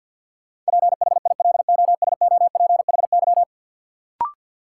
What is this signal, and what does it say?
Morse code, heard as a keyed tone, about 50 wpm, 700 Hz. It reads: PHILOSOPHY
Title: Sentences from Top 900 Words - New Word Drill - 50wpm